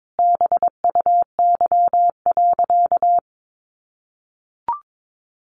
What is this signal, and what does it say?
Morse code, heard as a keyed tone, about 22 wpm, 700 Hz. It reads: BUY.
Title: Sentences from Top 1300 Words 22wpm